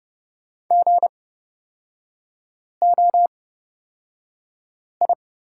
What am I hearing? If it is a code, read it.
Morse code, heard as a keyed tone, about 30 wpm, spaced as for 5 wpm, 700 Hz. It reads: GOI
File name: Lesson 36e - BT - Three Characters 30wpm